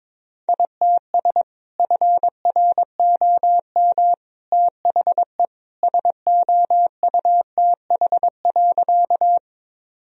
Morse code, heard as a keyed tone, about 22 wpm, 700 Hz. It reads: ITS FROM THE SOUTH.
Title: Sentences from Top 1000 Words - Review Part 2 - 22wpm